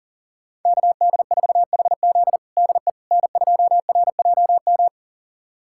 Morse code, heard as a keyed tone, about 37 wpm, 700 Hz. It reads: KD4HZ DE N2RJM
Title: USA Callsign DE Callsign - 5 Characters - 40wpm